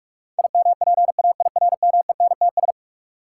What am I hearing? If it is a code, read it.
Morse code, heard as a keyed tone, about 46 wpm, 700 Hz. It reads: IMPAIRMENTS
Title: Sets of 5 Words Encore - New Word Drill - 45wpm